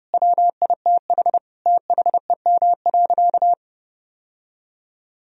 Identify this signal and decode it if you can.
Morse code, heard as a keyed tone, about 30 wpm, 700 Hz. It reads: WITH THEM.
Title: Sentences from Top 1000 Words - Review Part 4 - 30wpm